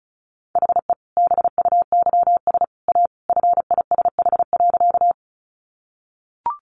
Morse code, heard as a keyed tone, about 34 wpm, 700 Hz. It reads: HE BUYS A FISH.
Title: Easy Phrases 35wpm